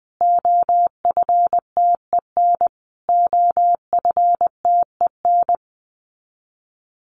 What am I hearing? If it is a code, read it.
Morse code, heard as a keyed tone, about 20 wpm, 700 Hz. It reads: OFTEN OFTEN
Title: Top 500 Words Repeated 20wpm